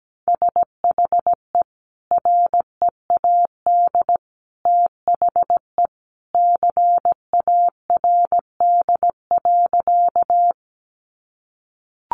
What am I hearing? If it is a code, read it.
Morse code, heard as a keyed tone, about 17 wpm, 700 Hz. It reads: SHE READ THE CARD.
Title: Sentences from Top 600 Words 17wpm